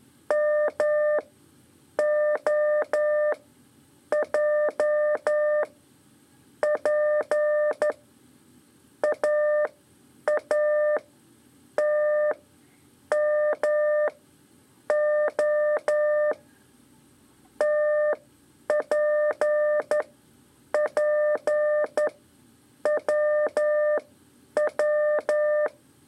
Morse code, heard as a keyed tone, about 12 wpm, spaced as for 8 wpm, 600 Hz. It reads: MOJ P AATMO TPPWW
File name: 3 lezione CW 2 AIR